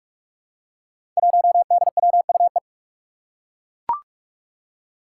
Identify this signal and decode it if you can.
Morse code, heard as a keyed tone, about 41 wpm, 700 Hz. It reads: 1DWUE
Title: Five Challenging Letter-Number Combinations 45wpm